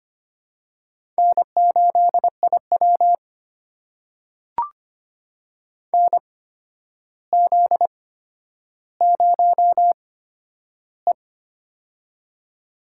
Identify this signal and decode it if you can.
Morse code, heard as a keyed tone, about 25 wpm, spaced as for 11 wpm, 700 Hz. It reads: N8IW NZ0E
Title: USA Call Signs - 4 Characters - 8x Character Spacing - 25wpm